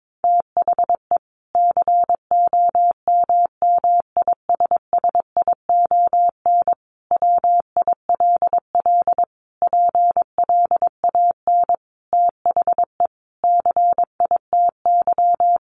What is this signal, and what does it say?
Morse code, heard as a keyed tone, about 22 wpm, 700 Hz. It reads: THE COMMISSION WILL PLAN THE CITY
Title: Sentences from Top 1400 Words 22wpm